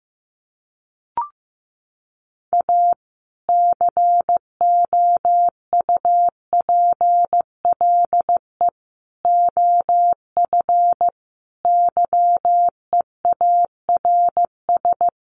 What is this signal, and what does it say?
Morse code, heard as a keyed tone, about 15 wpm, 700 Hz. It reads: A COUPLE OF YEARS
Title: Sets of 5 Words 15wpm